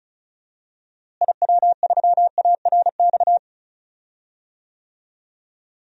Morse code, heard as a keyed tone, about 34 wpm, 700 Hz. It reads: IW3ARX
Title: DXCC-100 Call Signs - 6 Characters - Repeated 3x - 35wpm